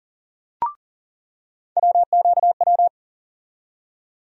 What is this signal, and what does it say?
Morse code, heard as a keyed tone, about 40 wpm, 700 Hz. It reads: WQW
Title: USA Call Sign Postfixes - 3 Letters 40wpm